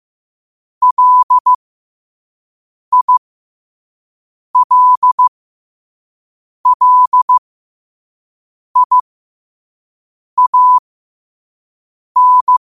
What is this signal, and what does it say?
Morse code, heard as a keyed tone, about 15 wpm, spaced as for 5 wpm, 1000 Hz. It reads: LILLIAN
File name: qotdfiles / 5wpm